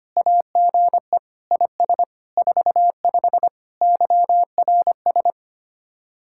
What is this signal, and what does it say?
Morse code, heard as a keyed tone, about 32 wpm, 700 Hz. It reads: AGE IS 45 YRS